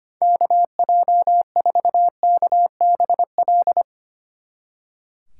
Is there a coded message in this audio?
Yes: Morse code from 0.2 to 3.8 s, heard as a keyed tone, about 32 wpm, 700 Hz. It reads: KJ4KBL